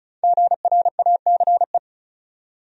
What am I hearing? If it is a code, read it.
Morse code, heard as a keyed tone, about 36 wpm, 700 Hz. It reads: GRACE
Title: USA Common Names - Max 5 Letters - 35wpm